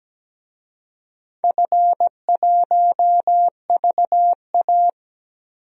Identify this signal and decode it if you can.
Morse code, heard as a keyed tone, about 17 wpm, 700 Hz. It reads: F1VA
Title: DXCC-100 Call Signs - 4 Characters - 17wpm